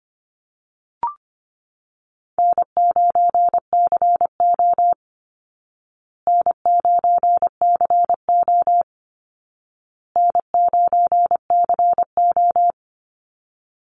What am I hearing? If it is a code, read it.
Morse code, heard as a keyed tone, about 25 wpm, 700 Hz. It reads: N9CO N9CO N9CO
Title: Easy Call Signs Repeated 3x - 25wpm